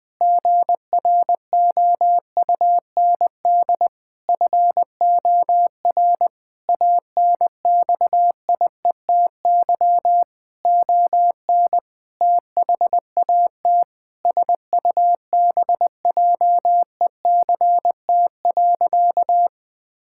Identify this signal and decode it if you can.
Morse code, heard as a keyed tone, about 20 wpm, 700 Hz. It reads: GROUND FOR ANXIETY ON THAT SUBJECT.
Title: Sentences from Top 1600 Words 20wpm